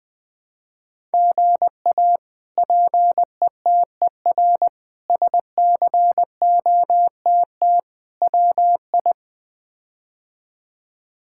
Morse code, heard as a keyed tone, about 20 wpm, 700 Hz. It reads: GA PETER SCOTT WI
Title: K1USN SST QSO Practice - 20wpm